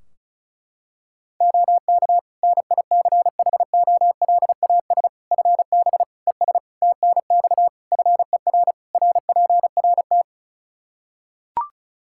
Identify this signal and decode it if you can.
Morse code, heard as a keyed tone, about 34 wpm, 700 Hz. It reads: OK NICHOLAS FB ES TNX FER RPRT
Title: Practice QSO - Exchange 2 - Info - 35wpm